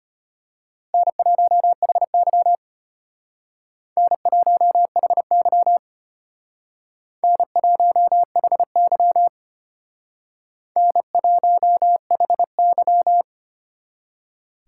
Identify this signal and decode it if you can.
Morse code, heard as a keyed tone, about 31 wpm, 700 Hz. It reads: N1HY N1HY N1HY N1HY
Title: Speed Racing - USA 1x2 Call Signs 25wpm